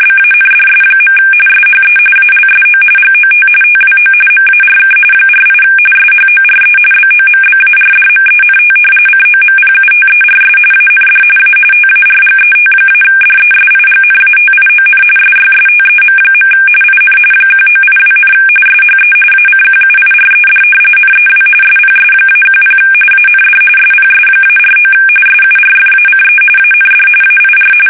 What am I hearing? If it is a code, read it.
RTTY, about 45.45 baud. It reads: BANDS. SHE ALSO GOT TWO MARK 48 SHORE BOMBARDMENT COMPUTERS AND A TARGET-DESIGNATION SYSTEM. IN 1982, SHE ACQUIRED NEW SPS-10 AND SPS-49 SURFACE AND AIR
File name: fsk850 / battleshipnj